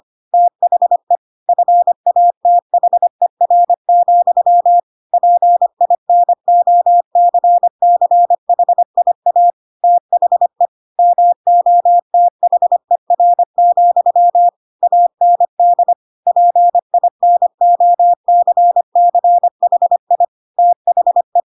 Morse code, heard as a keyed tone, about 25 wpm, 700 Hz. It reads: THE FATHER, PINOCCHIA THE MOTHER, AND PINOCCHI THE
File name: Pinocchio / Pinocchio ch3 25 25 ik1hge